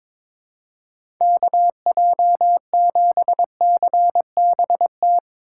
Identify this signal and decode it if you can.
Morse code, heard as a keyed tone, about 22 wpm, 700 Hz. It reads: KJ7CBT
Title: USA Call Signs - 8x Character Spacing - 22wpm